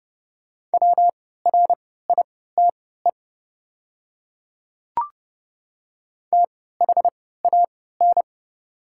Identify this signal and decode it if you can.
Morse code, heard as a keyed tone, about 29 wpm, spaced as for 17 wpm, 690 Hz. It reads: WRITE THAN
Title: 3x Character Spacing - Top 100 Words - 30wpm